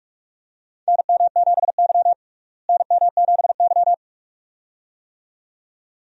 Morse code, heard as a keyed tone, about 46 wpm, 700 Hz. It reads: NM7Y NM7Y
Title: USA Call Signs - 4 Characters - Repeated 2x - 45wpm